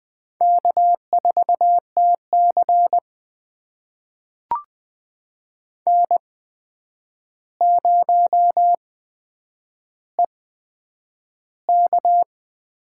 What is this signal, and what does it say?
Morse code, heard as a keyed tone, about 20 wpm, spaced as for 9 wpm, 700 Hz. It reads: K4TC N0EK
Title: USA Call Signs - 4 Characters - 8x Character Spacing - 20wpm